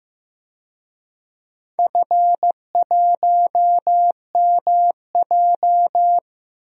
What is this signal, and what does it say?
Morse code, heard as a keyed tone, about 15 wpm, 700 Hz. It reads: F1MJ
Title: DXCC-100 Call Signs - 4 Characters - Repeated 3x - 15wpm